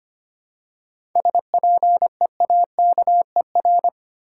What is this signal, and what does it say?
Morse code, heard as a keyed tone, about 25 wpm, 690 Hz. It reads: SPEAKER
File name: practice / 25wpm / Easy Words with Hints - 25wpm